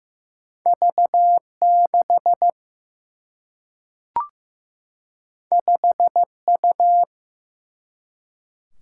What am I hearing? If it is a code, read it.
Morse code, heard as a keyed tone, about 15 wpm, 700 Hz. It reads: V 6 5 U
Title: Single-Letter Single-Number 15wpm